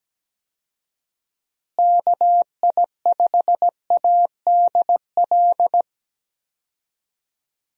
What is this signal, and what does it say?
Morse code, heard as a keyed tone, about 17 wpm, 700 Hz. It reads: KI5ADL
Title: USA Call Signs - 6 Characters - Repeated 3x - 17wpm